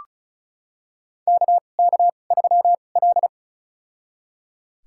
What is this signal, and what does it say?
Morse code, heard as a keyed tone, about 36 wpm, spaced as for 25 wpm, 700 Hz. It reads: KK3L
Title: USA Call Signs - 4 Characters - 2x Character Spacing - 35wpm